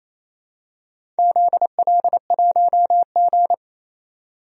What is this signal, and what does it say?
Morse code, heard as a keyed tone, about 28 wpm, 700 Hz. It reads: ZL1G
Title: DXCC-100 Call Signs - 4 Characters - 28wpm